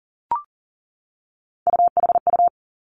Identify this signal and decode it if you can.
Morse code, heard as a keyed tone, about 38 wpm, 700 Hz. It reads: UHU
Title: Three Challenging Letter-Number Combinations 40wpm